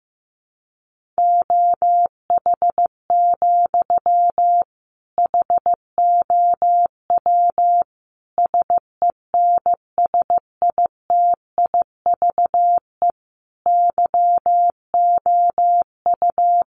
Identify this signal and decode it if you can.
Morse code, heard as a keyed tone, about 14 wpm, 700 Hz. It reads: OH, HOW SENSITIVE YOU
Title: Sentences from Top 1400 Words 15wpm